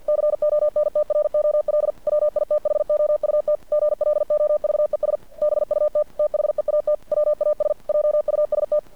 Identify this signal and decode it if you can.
Morse code, heard as a keyed tone, about 50 wpm, 600 Hz. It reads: CONTROL WITHOUT GLOVES BUT THEAT WAS JUST
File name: archived / CRSnet